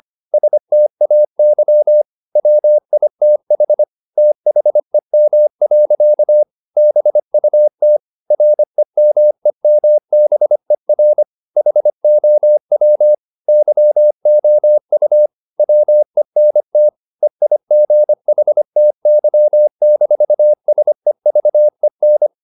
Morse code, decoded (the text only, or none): STAY WITH THEM. BUT REMEMBER HOW YOU WENT EIGHTY-SEVEN